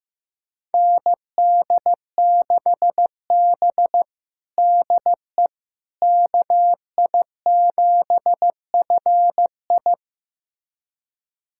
Morse code, heard as a keyed tone, about 15 wpm, 700 Hz. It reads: ND6B DE KI7FI
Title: USA Callsign DE Callsign 15wpm